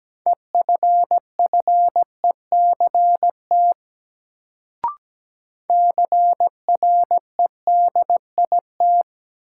Morse code, heard as a keyed tone, about 17 wpm, 700 Hz. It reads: EFFECT CREDIT
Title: Sentences from Top 1000 Words - New Word Drill - 17wpm